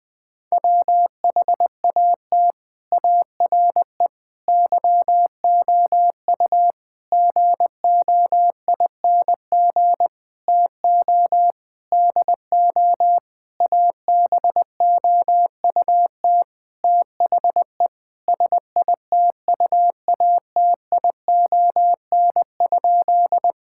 Morse code, heard as a keyed tone, about 20 wpm, 700 Hz. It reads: WHAT ARE YOU GOING TO DO ABOUT THE SITUATION?
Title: Sentences with 8-Word Phrases - 20wpm